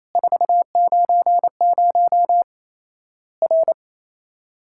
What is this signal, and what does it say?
Morse code, heard as a keyed tone, about 27 wpm, 700 Hz. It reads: 490 R
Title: Worked All Europe DX Contest - QTC Exchange - 28wpm